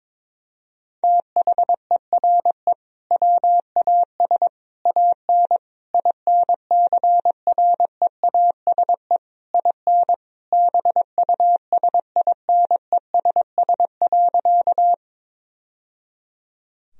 Morse code, heard as a keyed tone, about 22 wpm, 700 Hz. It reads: THERE WAS AN INCREASE IN BUSINESS.
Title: Sentences from Top 1000 Words - Review Part 4 - 22wpm